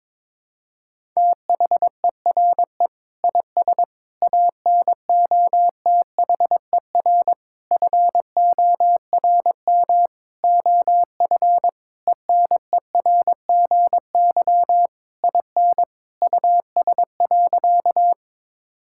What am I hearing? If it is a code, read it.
Morse code, heard as a keyed tone, about 22 wpm, 700 Hz. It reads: THERE IS ANOTHER FORM OF ENERGY IN US.
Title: Sentences from Top 1000 Words - Review Part 6 - 22wpm